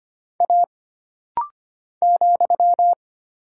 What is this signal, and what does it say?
Morse code, heard as a keyed tone, about 25 wpm, 700 Hz. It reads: A ,
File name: Single Character (Letter:Number:Punctuation:Prosign) - Rapid-Fire - 25wpm